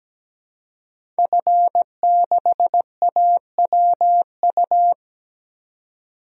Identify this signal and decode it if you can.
Morse code, heard as a keyed tone, about 17 wpm, 700 Hz. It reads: F6AWU
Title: DXCC-100 Call Signs - 5 Characters - 17wpm